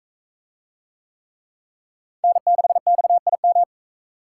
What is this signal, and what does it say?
Morse code, heard as a keyed tone, about 41 wpm, 700 Hz. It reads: N6XIM